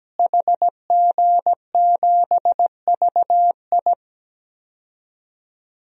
Morse code, heard as a keyed tone, about 17 wpm, 700 Hz. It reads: HG7VI
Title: DXCC-100 Call Signs - 5 Characters - Repeated 3x - 17wpm